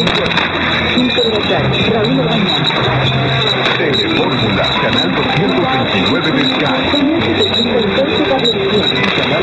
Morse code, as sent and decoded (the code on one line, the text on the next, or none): none